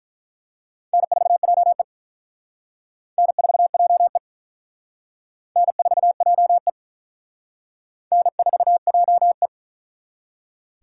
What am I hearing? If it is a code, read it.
Morse code, heard as a keyed tone, about 43 wpm, 700 Hz. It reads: N4JE N4JE N4JE N4JE
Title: Speed Racing - USA 1x2 Call Signs 35wpm